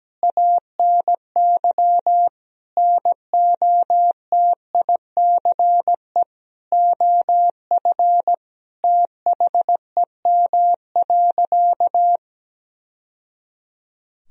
Morse code, heard as a keyed tone, about 17 wpm, 700 Hz. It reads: ANY NOTICE OF THEM.
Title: Sentences from Top 1000 Words - Review Part 2 - 17wpm